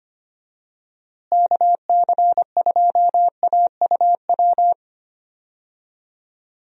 Morse code, heard as a keyed tone, about 25 wpm, 700 Hz. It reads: KC2AUW
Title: USA Call Signs - 6 Characters - Repeated 2x - 25wpm